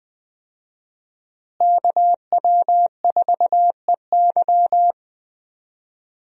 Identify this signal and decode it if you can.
Morse code, heard as a keyed tone, about 20 wpm, 700 Hz. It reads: KW4EY
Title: USA Call Signs - 5 Characters - Repeated 3x - 20wpm